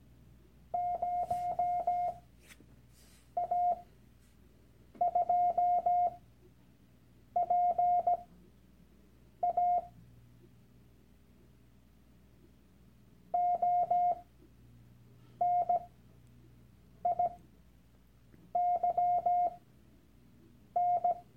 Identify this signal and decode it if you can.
Morse code, heard as a keyed tone, about 17 wpm, spaced as for 6 wpm, 710 Hz. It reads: 0A2PA ONIYN
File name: Workshop Dateien / CW-Online-Kurs / Lektion 001